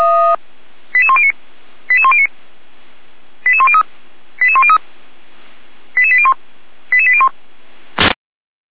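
Five-tone selective call (ZVEI), heard as quick runs of five tones, tones of about 70 ms. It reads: 88109 88109 88103 88103 88801 88801
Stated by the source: ZVEI five-tone selective call received with an SDR.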